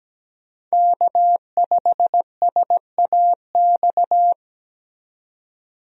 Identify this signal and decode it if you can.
Morse code, heard as a keyed tone, about 17 wpm, 700 Hz. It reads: K5SAX
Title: USA Call Signs - 5 Characters - 17wpm